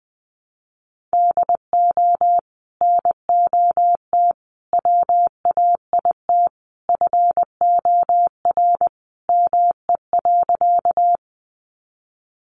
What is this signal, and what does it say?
Morse code, heard as a keyed tone, about 20 wpm, 700 Hz. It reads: DO NOT WAIT FOR ME.
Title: Sentences from Top 600 Words 20wpm